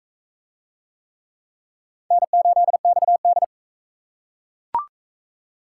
Morse code, heard as a keyed tone, about 40 wpm, 700 Hz. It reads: N8XD